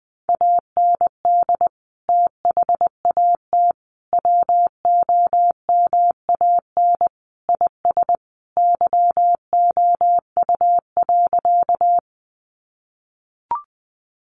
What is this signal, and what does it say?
Morse code, heard as a keyed tone, about 20 wpm, 700 Hz. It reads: AND THAT WOMAN IS YOU.
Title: Sentences from Top 1100 Words 20wpm